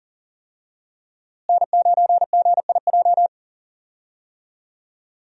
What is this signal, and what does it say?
Morse code, heard as a keyed tone, about 40 wpm, 700 Hz. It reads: N9GIJ
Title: USA Call Signs - 5 Characters - Repeated 2x - 40wpm